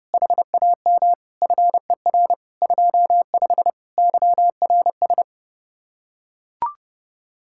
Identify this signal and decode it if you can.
Morse code, heard as a keyed tone, about 30 wpm, 700 Hz. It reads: HAM FER 25 YRS